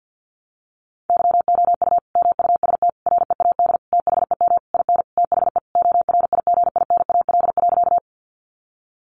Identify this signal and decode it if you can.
Morse code, heard as a keyed tone, about 50 wpm, 700 Hz. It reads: YOU MUST READ THEM IN THE ORIGINAL.
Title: Sentences from Top 900 Words 50wpm